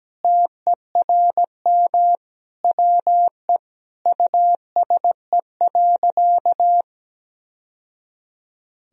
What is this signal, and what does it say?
Morse code, heard as a keyed tone, about 17 wpm, 700 Hz. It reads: TERM WE USE.
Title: Sentences from Top 1000 Words - Review Part 4 - 17wpm